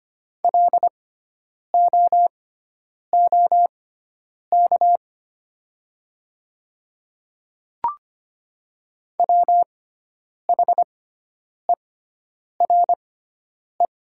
Morse code, heard as a keyed tone, about 25 wpm, spaced as for 9 wpm, 700 Hz. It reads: LOOK WHERE
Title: 6x Character Spacing - Top 300 Words - 25wpm